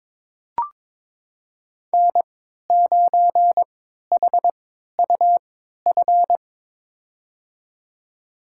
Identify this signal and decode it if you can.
Morse code, heard as a keyed tone, about 22 wpm, spaced as for 13 wpm, 700 Hz. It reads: N9HUF
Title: USA Call Signs - 3x Character Spacing - 22wpm